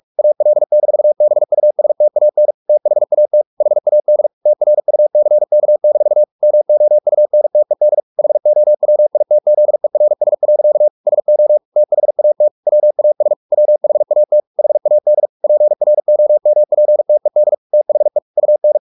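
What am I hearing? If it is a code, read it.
Morse code, heard as a keyed tone, about 44 wpm, 600 Hz. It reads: AL-BUSTAN THAT HAD TRUCK-MOUNTED HOWITZERS. SO THAT WAS WHAT HAD PROMPTED THE UN
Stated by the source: Morse practice at 45 wpm, the novel Never by Ken Follett in English, from CW med Gnister.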